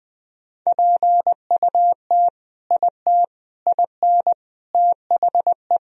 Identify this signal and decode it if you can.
Morse code, heard as a keyed tone, about 21 wpm, 700 Hz. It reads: PUT IT IN THE
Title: Sentences from Top 1000 Words - Review Part 4 - 20wpm